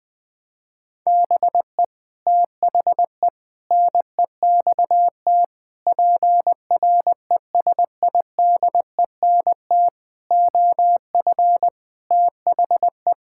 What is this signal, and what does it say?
Morse code, heard as a keyed tone, about 20 wpm, 700 Hz. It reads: BE THE NEXT PRESIDENT OF THE
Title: Sets of 8 Words - 20wpm